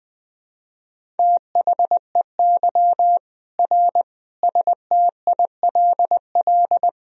Morse code, heard as a keyed tone, about 21 wpm, 700 Hz. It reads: THEY R STILL
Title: Sentences from Top 400 Words and Abbreviations 20wpm